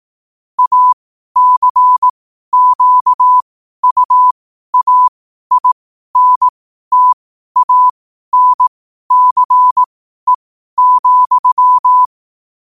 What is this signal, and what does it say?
Morse code, heard as a keyed tone, about 18 wpm, spaced as for 12 wpm, 1000 Hz. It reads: ACQUAINTANCE,
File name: qotdfiles / 12wpm